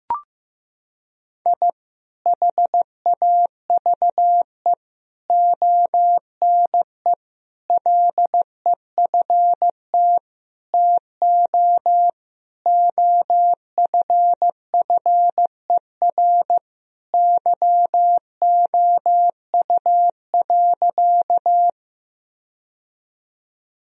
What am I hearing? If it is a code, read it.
Morse code, heard as a keyed tone, about 16 wpm, 700 Hz. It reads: I HAVE ONE LEFT TO OFFER YOU.
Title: Sentences from Top 1000 Words - Review Part 4 - 15wpm